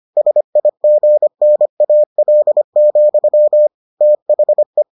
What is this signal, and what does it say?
Morse code, heard as a keyed tone, about 25 wpm, 600 Hz. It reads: SIGNAL, THE